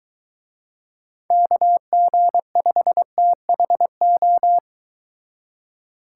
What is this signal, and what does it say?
Morse code, heard as a keyed tone, about 23 wpm, 700 Hz. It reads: KG5THO